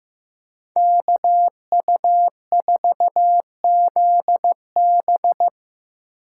Morse code, heard as a keyed tone, about 18 wpm, 700 Hz. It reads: KU4ZB
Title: Speed Racing - USA 2x2 Call Signs 15wpm